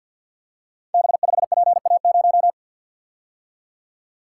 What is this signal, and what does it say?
Morse code, heard as a keyed tone, about 53 wpm, 700 Hz. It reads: B5PA0
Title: Code Groups - 5 Letter:Numbers - 50wpm